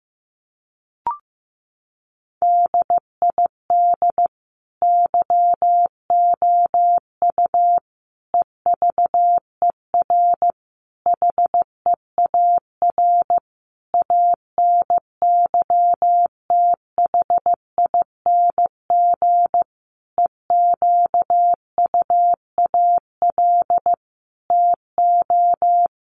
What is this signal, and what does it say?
Morse code, heard as a keyed tone, about 14 wpm, 700 Hz. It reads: DID YOU EVER HEAR ANYTHING EQUAL TO
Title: Sentences from Top 1400 Words 15wpm